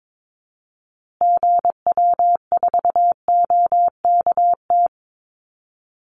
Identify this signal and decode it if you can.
Morse code, heard as a keyed tone, about 22 wpm, 700 Hz. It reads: GW4OKT
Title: Call Signs Repeated 2x - 22wpm